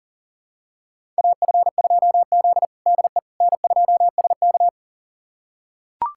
Morse code, heard as a keyed tone, about 40 wpm, 700 Hz. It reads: AF2Z DE N2SK
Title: USA Callsign DE Callsign - 4 Characters - 40wpm